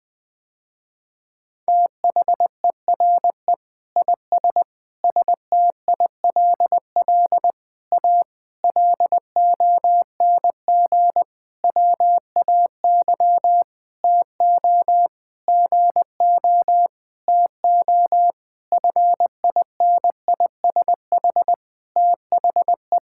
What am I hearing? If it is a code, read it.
Morse code, heard as a keyed tone, about 20 wpm, 700 Hz. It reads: THERE IS STILL A LONG WAY TO GO TO FINISH THE
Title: Sentences with 8-Word Phrases - Spoken Beginning - 20wpm